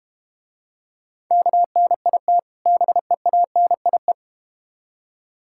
Morse code, heard as a keyed tone, about 32 wpm, 700 Hz. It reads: KNIT BEANIE